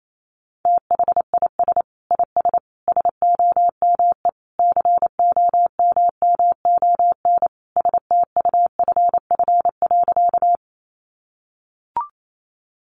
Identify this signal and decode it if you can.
Morse code, heard as a keyed tone, about 28 wpm, 700 Hz. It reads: THIS IS SOME COMMON STUFF.
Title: Sentences from Top 1400 Words 28wpm